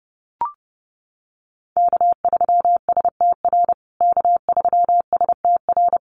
Morse code, heard as a keyed tone, about 30 wpm, 700 Hz. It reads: K3STR K3STR
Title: USA Call Signs Repeated 2x 30wpm